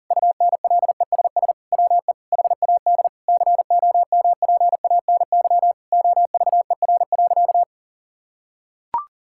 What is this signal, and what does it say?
Morse code, heard as a keyed tone, about 37 wpm, 700 Hz. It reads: UNLESS WE HAD COMPANY OVER.
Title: Sentences from Top 1000 Words - Review Part 6 - 40wpm